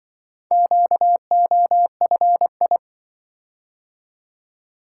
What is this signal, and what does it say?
Morse code, heard as a keyed tone, about 24 wpm, 700 Hz. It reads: QOFI
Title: Speed-Racing - Four Letter-Number Combinations - 20wpm